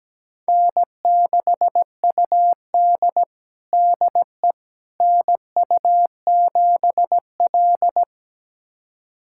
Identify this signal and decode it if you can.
Morse code, heard as a keyed tone, about 17 wpm, 700 Hz. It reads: N6UD DE NU7L
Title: USA Callsign DE Callsign - 4 Characters - 17wpm